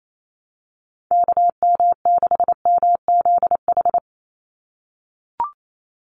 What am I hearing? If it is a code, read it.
Morse code, heard as a keyed tone, about 28 wpm, 700 Hz. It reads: KM6MZH
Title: USA Call Signs 28wpm